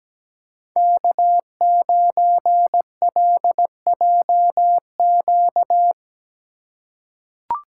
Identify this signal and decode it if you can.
Morse code, heard as a keyed tone, about 17 wpm, 700 Hz. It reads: K9LJQ
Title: USA Call Signs - 6x Character Spacing - 17wpm